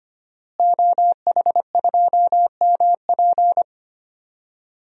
Morse code, heard as a keyed tone, about 25 wpm, 700 Hz. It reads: OH2MP
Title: DXCC-100 Call Signs - 5 Characters - Repeated 2x - 25wpm